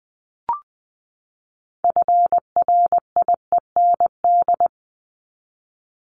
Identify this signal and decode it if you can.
Morse code, heard as a keyed tone, about 20 wpm, 700 Hz. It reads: FRIEND
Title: Top 500 Words 20wpm